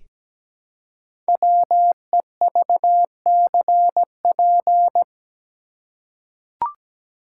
Morse code, heard as a keyed tone, about 17 wpm, 700 Hz. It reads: WEVCP
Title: Code Groups - 5 Letter:Numbers - 17wpm